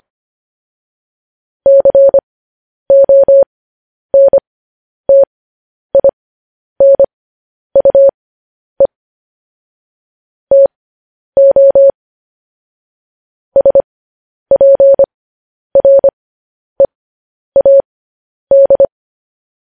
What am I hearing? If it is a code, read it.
Morse code, heard as a keyed tone, about 25 wpm, spaced as for 10 wpm, 550 Hz. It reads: CONTINUE TO SPREAD